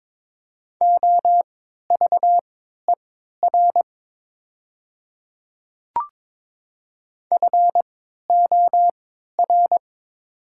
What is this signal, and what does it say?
Morse code, heard as a keyed tone, about 22 wpm, spaced as for 12 wpm, 700 Hz. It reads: OVER FOR
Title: 3x Character Spacing - Top 100 Words - 22wpm